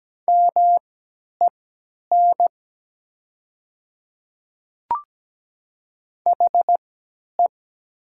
Morse code, heard as a keyed tone, about 17 wpm, spaced as for 10 wpm, 700 Hz. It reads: MEN HE
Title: 3x Character Spacing - Top 200 Words - 17wpm